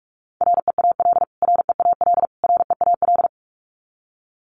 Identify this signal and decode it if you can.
Morse code, heard as a keyed tone, about 44 wpm, 700 Hz. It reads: REAL REAL REAL
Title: Top 300 Words Repeated 45wpm